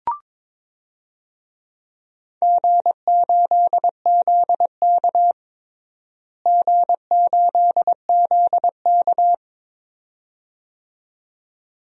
Morse code, heard as a keyed tone, about 22 wpm, 700 Hz. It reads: G8ZK G8ZK
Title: DXCC-100 Call Signs - 4 Characters - Repeated 2x - 22wpm